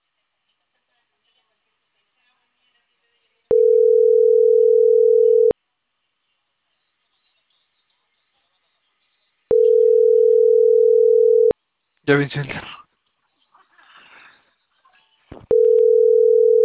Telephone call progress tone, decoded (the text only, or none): ringback tone